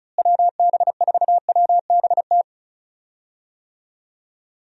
Morse code, heard as a keyed tone, about 36 wpm, 700 Hz. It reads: WB4WBT
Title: USA Call Signs - 6 Characters - Repeated 3x - 35wpm